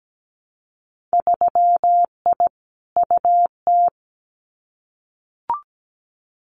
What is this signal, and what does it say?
Morse code, heard as a keyed tone, about 17 wpm, 700 Hz. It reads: 3I UT
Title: Winter Field Day Exchanges Encore 17wpm